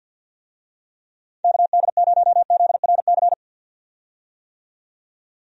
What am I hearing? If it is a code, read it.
Morse code, heard as a keyed tone, about 51 wpm, 700 Hz. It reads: KD0ZRC
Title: USA Call Signs - 6 Characters - 50wpm